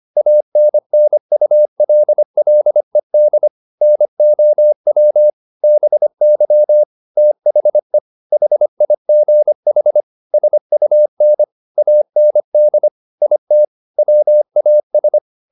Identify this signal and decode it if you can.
Morse code, heard as a keyed tone, about 25 wpm, 600 Hz. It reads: ANNULLED NOW BY THE HIGH SUN AND IT WAS
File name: the old man and the sea part 1 25wpm